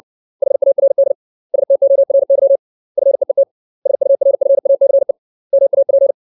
Morse code, heard as a keyed tone, about 60 wpm, 550 Hz. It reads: HARD STORY FEET SURFACE MAP
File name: 632 words / 632 Words 60WPM